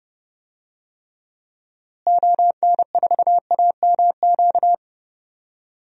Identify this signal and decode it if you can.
Morse code, heard as a keyed tone, about 32 wpm, 700 Hz. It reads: ON4AMQ
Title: DXCC-100 Call Signs - 6 Characters - Repeated 2x - 30wpm